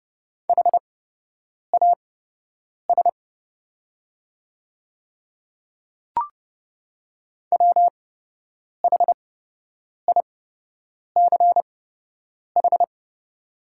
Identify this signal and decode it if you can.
Morse code, heard as a keyed tone, about 30 wpm, spaced as for 8 wpm, 700 Hz. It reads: HAS WHICH
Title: 8x Character Spacing - Top 100 Words - 30wpm